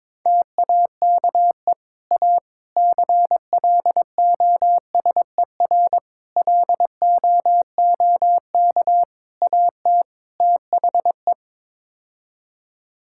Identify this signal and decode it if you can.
Morse code, heard as a keyed tone, about 22 wpm, 700 Hz. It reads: TAKE A CLOSER LOOK AT THE
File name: practice / 22wpm / Sets of 6 Words - 22wpm